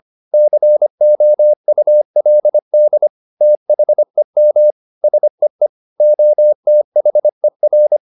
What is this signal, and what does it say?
Morse code, heard as a keyed tone, about 25 wpm, 600 Hz. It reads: COULD THEM SEE OTHER
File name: Top-100-Words-25